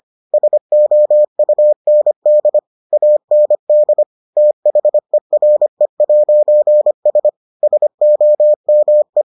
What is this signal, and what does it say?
Morse code, heard as a keyed tone, about 25 wpm, 600 Hz. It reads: SOUND AND THERE'S SOME